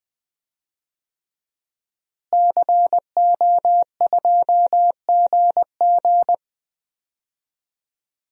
Morse code, heard as a keyed tone, about 20 wpm, 700 Hz. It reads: CO2GG
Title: DXCC-100 Call Signs - 5 Characters - Repeated 3x - 20wpm